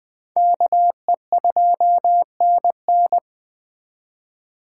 Morse code, heard as a keyed tone, about 21 wpm, 700 Hz. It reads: KE2NN